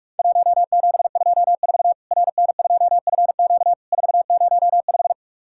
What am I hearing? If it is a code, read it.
Morse code, heard as a keyed tone, about 44 wpm, 700 Hz. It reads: 1724 RN2FQ 405
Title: Worked All Europe DX Contest - QTC Exchange - 45wpm